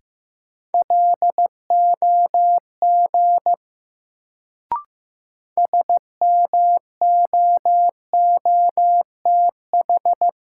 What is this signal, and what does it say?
Morse code, heard as a keyed tone, about 15 wpm, 700 Hz. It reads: LOG SMOOTH
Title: Sentences from Top 1400 Words - New Word Drill Alt - 15wpm